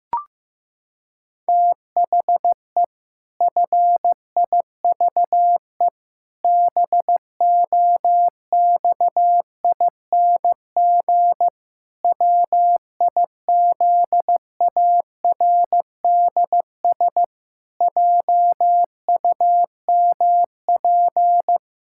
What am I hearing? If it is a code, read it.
Morse code, heard as a keyed tone, about 15 wpm, 700 Hz. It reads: THE FIVE BOXING WIZARDS JUMP
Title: Pangram Challenge v2 - 15wpm